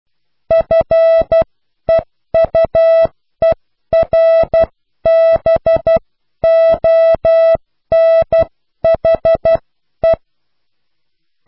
Morse code, heard as a keyed tone, about 12 wpm, spaced as for 10 wpm, 660 Hz. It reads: FEUERBONHE
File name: cw / pseudo worden(10)